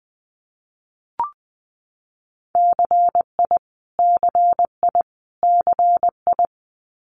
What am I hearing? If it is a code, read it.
Morse code, heard as a keyed tone, about 20 wpm, 700 Hz. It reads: CI CI CI
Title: Multi-letter Phonemes Repeated 3x 20wpm